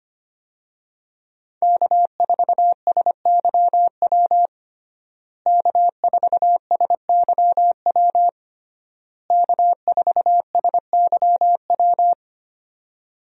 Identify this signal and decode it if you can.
Morse code, heard as a keyed tone, about 25 wpm, 700 Hz. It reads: K4SYW K4SYW K4SYW
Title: USA Call Signs - 5 Characters - Repeated 3x - 25wpm